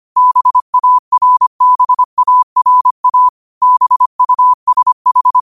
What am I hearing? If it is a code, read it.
Morse code, heard as a keyed tone, about 25 wpm, 1000 Hz. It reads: BARBARA BUSH
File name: qotdfiles / 25wpm